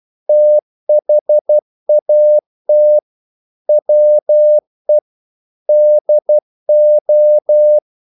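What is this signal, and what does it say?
Morse code, heard as a keyed tone, about 12 wpm, 600 Hz. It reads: THAT WE DO
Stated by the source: Morse practice sending English quotations.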